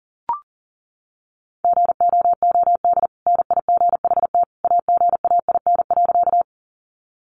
Morse code, heard as a keyed tone, about 40 wpm, 700 Hz. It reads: GOOD NIGHT AGAIN.
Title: Sentences from Top 300 Words 40wpm